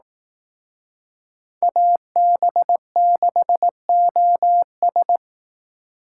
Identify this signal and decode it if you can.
Morse code, heard as a keyed tone, about 18 wpm, 700 Hz. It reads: AB6OS
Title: Speed Racing - USA Call Signs 15wpm